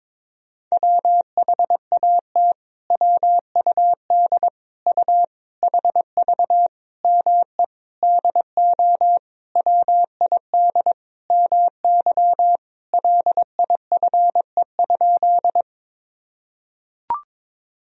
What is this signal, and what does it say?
Morse code, heard as a keyed tone, about 22 wpm, 700 Hz. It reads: WHAT WUD U HV ME DO WID MY LIFE?
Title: Sentences from Top 300 Words and Abbreviations 22wpm